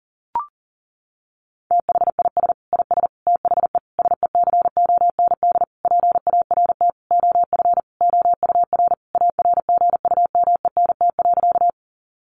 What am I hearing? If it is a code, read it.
Morse code, heard as a keyed tone, about 37 wpm, 700 Hz. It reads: THIS IS THE SECOND PART OF OUR ARGUMENT.
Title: Sentences from Top 1400 Words 40wpm